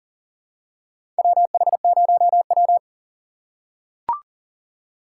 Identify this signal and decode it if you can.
Morse code, heard as a keyed tone, about 40 wpm, 690 Hz. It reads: WH0W
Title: USA Call Signs - 4 Characters - 2x Character Spacing - 40wpm